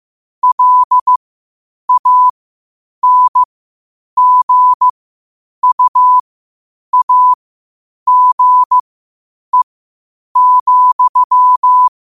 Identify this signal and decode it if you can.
Morse code, heard as a keyed tone, about 15 wpm, spaced as for 8 wpm, 1000 Hz. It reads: LANGUAGE,
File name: qotdfiles / 8wpm